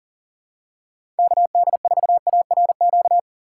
Morse code, heard as a keyed tone, about 40 wpm, 700 Hz. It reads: KD4ARQ
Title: USA Call Signs - 6 Characters - 3x Character Spacing - 40wpm